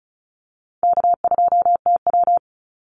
Morse code, heard as a keyed tone, about 34 wpm, 700 Hz. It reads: K2TW
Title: Easy Call Signs Repeated 2x - 35wpm